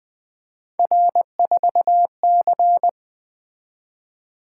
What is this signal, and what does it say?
Morse code, heard as a keyed tone, about 20 wpm, 700 Hz. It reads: R4C
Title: Three Letter-Number Combinations - 20wpm